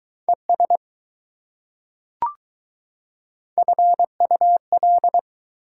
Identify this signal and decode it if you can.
Morse code, heard as a keyed tone, about 20 wpm, 700 Hz. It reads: ES FUL